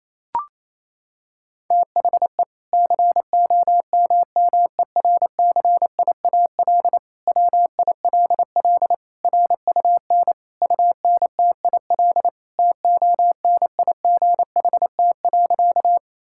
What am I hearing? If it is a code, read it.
Morse code, heard as a keyed tone, about 28 wpm, 700 Hz. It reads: THE COMMERCIAL WILL RUN UNTIL TONIGHT.
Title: Sentences from Top 1000 Words - Review Part 5 - 28wpm